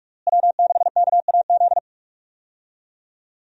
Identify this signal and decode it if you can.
Morse code, heard as a keyed tone, about 43 wpm, 700 Hz. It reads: W6KAZ
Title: USA Call Signs - 5 Characters - Repeated 3x - 45wpm